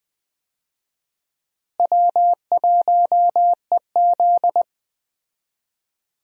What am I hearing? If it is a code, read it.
Morse code, heard as a keyed tone, about 21 wpm, 700 Hz. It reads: W1EZ
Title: USA Call Signs - 4 Characters - Repeated 2x - 20wpm